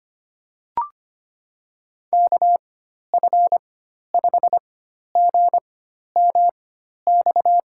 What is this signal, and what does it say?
Morse code, heard as a keyed tone, about 25 wpm, spaced as for 12 wpm, 700 Hz. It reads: KF5GMX
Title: USA Call Signs - 4x Character Spacing - 25wpm